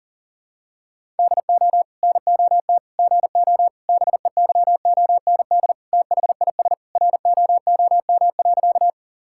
Morse code, heard as a keyed tone, about 40 wpm, 700 Hz. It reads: DO NOT GO BEYOND THIS ROOM.
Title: Sentences from Top 1000 Words - Review Part 6 - 40wpm